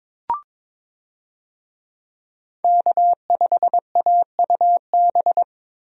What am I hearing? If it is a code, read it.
Morse code, heard as a keyed tone, about 22 wpm, 700 Hz. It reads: K5AUB
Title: USA Call Signs - 5 Characters - Repeated 2x - 22wpm